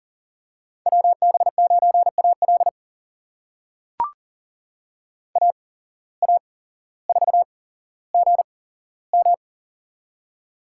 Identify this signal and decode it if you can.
Morse code, heard as a keyed tone, about 40 wpm, spaced as for 17 wpm, 700 Hz. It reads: WB9AL AA4GM
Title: USA Call Signs - 5 Characters - 8x Character Spacing - 40wpm